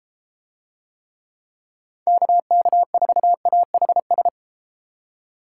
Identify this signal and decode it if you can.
Morse code, heard as a keyed tone, about 32 wpm, 700 Hz. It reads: KK4AHS